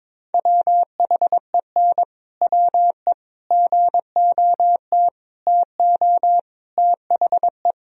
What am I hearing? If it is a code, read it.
Morse code, heard as a keyed tone, about 21 wpm, 700 Hz. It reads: WHEN WE GOT TO THE
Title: Sentences from Top 1000 Words - Review Part 1 - 22wpm